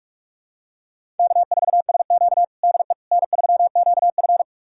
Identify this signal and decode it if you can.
Morse code, heard as a keyed tone, about 46 wpm, 700 Hz. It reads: K4SQ DE N3QF
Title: USA Callsign DE Callsign - 4 Characters - 45wpm